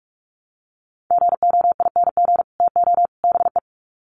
Morse code, heard as a keyed tone, about 44 wpm, 700 Hz. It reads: GOING TO BE
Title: Sets of 3 Words 45wpm